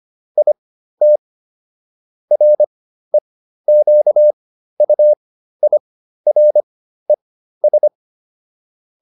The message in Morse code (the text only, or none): IT REQUIRES